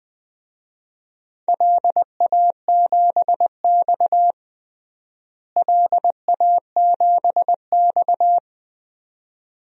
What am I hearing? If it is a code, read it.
Morse code, heard as a keyed tone, about 20 wpm, 700 Hz. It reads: LA7X LA7X
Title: DXCC-100 Call Signs - 4 Characters - Repeated 2x - 20wpm